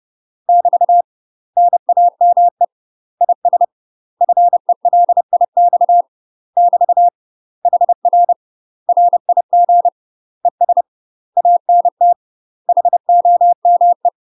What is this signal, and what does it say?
Morse code, heard as a keyed tone, about 30 wpm, 700 Hz. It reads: <BT> NAME IS FELIX <BT> HR RIG ES ANT HOME
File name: UK MASTER QSOs TRAINING / 30WPM / UK MASTER QSOs 0015